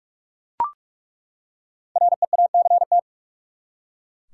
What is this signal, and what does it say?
Morse code, heard as a keyed tone, about 46 wpm, 700 Hz. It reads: REACT
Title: Five Letter Words - 45wpm